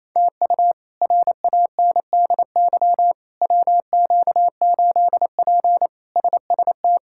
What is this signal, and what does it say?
Morse code, heard as a keyed tone, about 28 wpm, 700 Hz. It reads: TU RANDY WQ8P SST
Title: K1USN SST QSO Practice - 28wpm